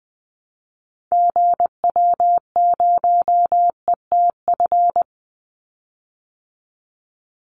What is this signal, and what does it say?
Morse code, heard as a keyed tone, about 20 wpm, 700 Hz. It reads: GW0ETF
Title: Call Signs Repeated 3x - 20wpm